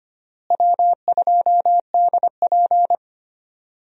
Morse code, heard as a keyed tone, about 25 wpm, 700 Hz. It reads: W2DP
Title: USA Call Signs - 4 Characters - Repeated 3x - 25wpm